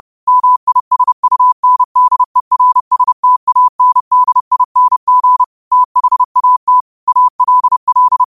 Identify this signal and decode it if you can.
Morse code, heard as a keyed tone, about 30 wpm, 1000 Hz. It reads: MISUNDERSTANDING THAT ALL
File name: qotdfiles / 30wpm